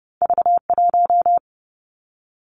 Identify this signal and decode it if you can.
Morse code, heard as a keyed tone, about 30 wpm, 700 Hz. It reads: V 1
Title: Two Challenging Letter-Number Combinations 30wpm